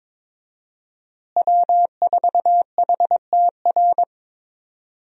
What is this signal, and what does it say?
Morse code, heard as a keyed tone, about 21 wpm, 700 Hz. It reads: W4HTR
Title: USA Call Signs - 5 Characters - Repeated 3x - 22wpm